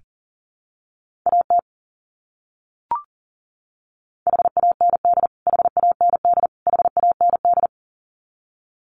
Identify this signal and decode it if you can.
Morse code, heard as a keyed tone, about 37 wpm, 700 Hz. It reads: AT HAND HAND HAND
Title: Top 300 Words Repeated 40wpm